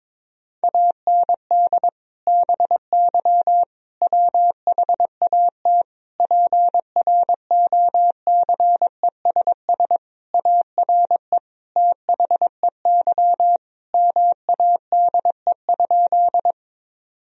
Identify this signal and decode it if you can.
Morse code, heard as a keyed tone, about 22 wpm, 700 Hz. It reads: AND BY WHAT PROCESS ARE THEY MADE?
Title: Sentences from Top 1000 Words - Review Part 3 - 22wpm